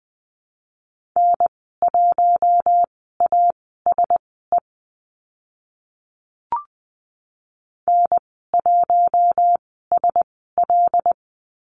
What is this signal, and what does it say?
Morse code, heard as a keyed tone, about 20 wpm, spaced as for 14 wpm, 700 Hz. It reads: N1ASE N1SL
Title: 076 Lesson 13g - Letter L - Call Signs 20wpm